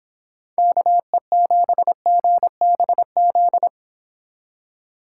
Morse code, heard as a keyed tone, about 32 wpm, 700 Hz. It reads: KE7GBZ